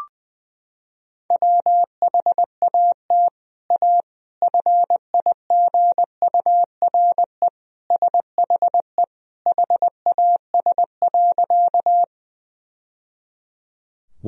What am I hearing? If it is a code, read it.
Morse code, heard as a keyed tone, about 21 wpm, 700 Hz. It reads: WHAT A FIGURE SHE HAS.
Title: Sentences from Top 1000 Words - Review Part 2 - 20wpm